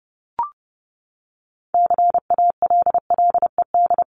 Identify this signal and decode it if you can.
Morse code, heard as a keyed tone, about 30 wpm, 700 Hz. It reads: CALLED
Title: ED Suffix - Top 100 Words with Base Words 30wpm